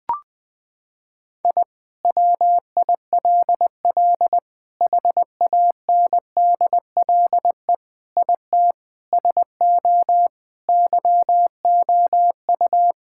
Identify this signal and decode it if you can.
Morse code, heard as a keyed tone, about 21 wpm, 700 Hz. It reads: I WILL HANDLE IT SO YOU
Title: Sentences with 7-Word Phrases - 20wpm